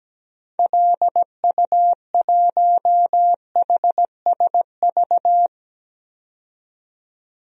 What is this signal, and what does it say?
Morse code, heard as a keyed tone, about 17 wpm, 700 Hz. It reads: LU1HSV
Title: DXCC-100 Call Signs - 6 Characters - 17wpm